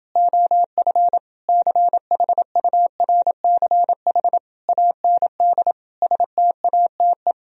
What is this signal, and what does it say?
Morse code, heard as a keyed tone, about 28 wpm, 700 Hz. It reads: OF CHURCH AND STATE